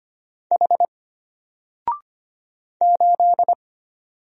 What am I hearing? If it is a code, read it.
Morse code, heard as a keyed tone, about 25 wpm, 700 Hz. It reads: H 8
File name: Single-Letter-Number-25wpm